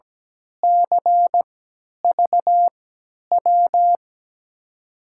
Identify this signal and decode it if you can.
Morse code, heard as a keyed tone, about 17 wpm, spaced as for 10 wpm, 700 Hz. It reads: CVW
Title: USA Call Signs - 6 Characters - 3x Character Spacing - 17wpm